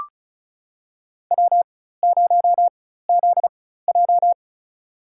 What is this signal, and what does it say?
Morse code, heard as a keyed tone, about 36 wpm, spaced as for 16 wpm, 700 Hz. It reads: W0ZJ
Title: USA Call Signs - 4x Character Spacing - 35wpm